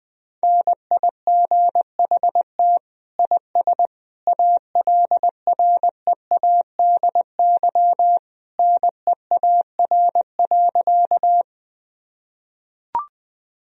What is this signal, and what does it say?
Morse code, heard as a keyed tone, about 20 wpm, 700 Hz. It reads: NIGHT IS ALREADY NEAR.
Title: Sentences from Top 1000 Words - Review Part 1 - 20wpm